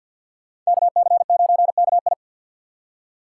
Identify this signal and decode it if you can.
Morse code, heard as a keyed tone, about 65 wpm, 700 Hz. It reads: KC9KI